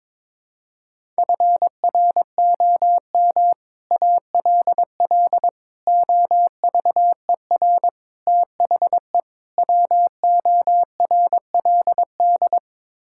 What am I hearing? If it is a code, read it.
Morse code, heard as a keyed tone, about 22 wpm, 700 Hz. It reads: FROM ALL OVER THE WORLD
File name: practice / 22wpm / Build A Phrase - 5-Word Phrases - 22wpm